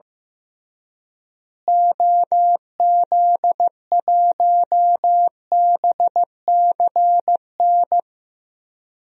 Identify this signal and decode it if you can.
Morse code, heard as a keyed tone, about 15 wpm, 700 Hz. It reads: OZ1BCN
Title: DXCC-100 Call Signs - 6 Characters - Repeated 3x - 15wpm